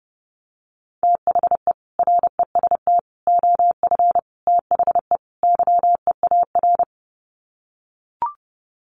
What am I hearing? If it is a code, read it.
Morse code, heard as a keyed tone, about 30 wpm, 700 Hz. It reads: THE REST OF THE YEAR
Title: Sets of 5 Words Encore 30wpm